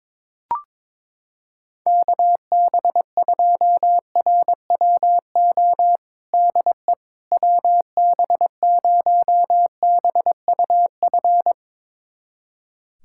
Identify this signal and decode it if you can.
Morse code, heard as a keyed tone, about 21 wpm, 700 Hz. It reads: KB2RWO DE WB0BUF
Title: USA Callsign DE Callsign - 6 Characters - 22wpm